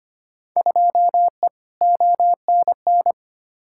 Morse code, heard as a keyed tone, about 25 wpm, 700 Hz. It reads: 2E ONN
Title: Field Day Exchanges Encore - 25wpm